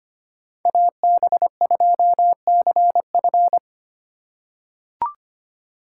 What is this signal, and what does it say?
Morse code, heard as a keyed tone, about 25 wpm, 700 Hz. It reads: AB2CF
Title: USA Call Signs - 5 Characters - 4x Character Spacing - 25wpm